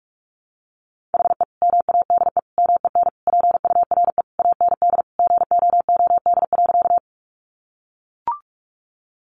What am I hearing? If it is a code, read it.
Morse code, heard as a keyed tone, about 44 wpm, 700 Hz. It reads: HE MADE MEN PURE AND GOOD.
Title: Sentences from Top 1500 Words 45wpm